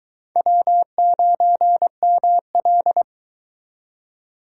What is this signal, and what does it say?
Morse code, heard as a keyed tone, about 22 wpm, 700 Hz. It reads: W9ML